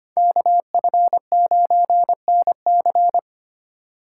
Morse code, heard as a keyed tone, about 25 wpm, 700 Hz. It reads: KF9NC
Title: USA Call Signs - 5 Characters - 3x Character Spacing - 25wpm